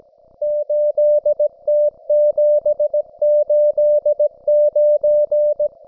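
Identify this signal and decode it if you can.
Morse code, heard as a keyed tone, about 17 wpm, 600 Hz. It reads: 8T789
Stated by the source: Russian military Morse station, M32, RDL, on 5268 kHz.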